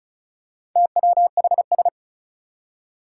Morse code, heard as a keyed tone, about 36 wpm, 700 Hz. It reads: TWHS
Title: Four Challenging Letter-Number Combinations 35wpm